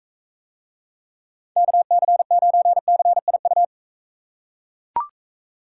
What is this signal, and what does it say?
Morse code, heard as a keyed tone, about 41 wpm, 700 Hz. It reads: KC9CIU